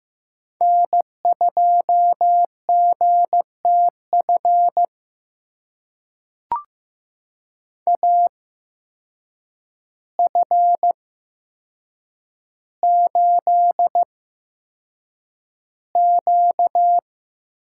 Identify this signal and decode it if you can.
Morse code, heard as a keyed tone, about 15 wpm, spaced as for 7 wpm, 700 Hz. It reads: N2GTF AF8Q
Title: USA Call Signs - 8x Character Spacing - 15wpm